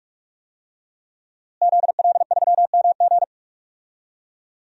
Morse code, heard as a keyed tone, about 46 wpm, 700 Hz. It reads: ZL3MG
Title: DXCC-100 Call Signs - 5 Characters - Repeated 2x - 45wpm